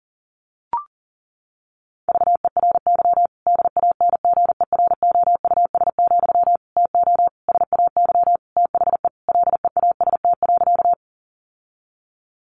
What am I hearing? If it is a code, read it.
Morse code, heard as a keyed tone, about 40 wpm, 700 Hz. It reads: VERY DANGEROUS, TO SAY THE LEAST.
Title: Sentences from Top 1200 Words 40wpm